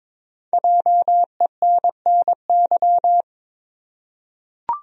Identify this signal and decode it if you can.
Morse code, heard as a keyed tone, about 22 wpm, 700 Hz. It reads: JENNY
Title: USA Common Names - Max 5 Letters - 22wpm